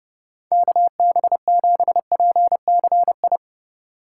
Morse code, heard as a keyed tone, about 29 wpm, 700 Hz. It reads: KB7PCI